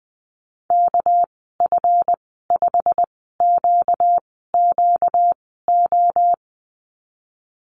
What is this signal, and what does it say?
Morse code, heard as a keyed tone, about 19 wpm, spaced as for 14 wpm, 700 Hz. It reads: KF5QQO
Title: 238 Lesson 38g - Letter q - Call Signs 20wpm